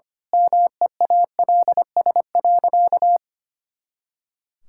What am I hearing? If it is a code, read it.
Morse code, heard as a keyed tone, about 25 wpm, 700 Hz. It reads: MEALS.
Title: Pangram Challenge v2 - 25wpm